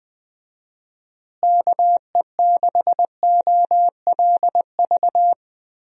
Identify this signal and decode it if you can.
Morse code, heard as a keyed tone, about 20 wpm, 700 Hz. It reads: KE6OLV
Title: USA Call Signs - 6 Characters - Repeated 3x - 20wpm